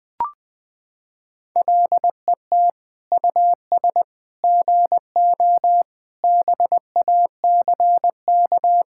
Morse code, heard as a keyed tone, about 20 wpm, 700 Hz. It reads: LET US GO BACK
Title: Sentences with 4-Word Phrases - 20wpm